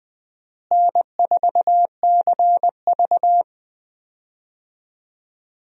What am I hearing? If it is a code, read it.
Morse code, heard as a keyed tone, about 20 wpm, 700 Hz. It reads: N4CV
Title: USA Call Signs - 4 Characters - Repeated 3x - 20wpm